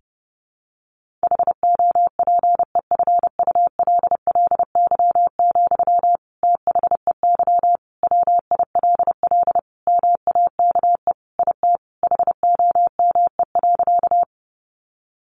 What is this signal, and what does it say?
Morse code, heard as a keyed tone, about 30 wpm, 700 Hz. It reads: HOPEFULLY, THEY WILL MAKE IT HOME.
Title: Sentences from Top 1500 Words 30wpm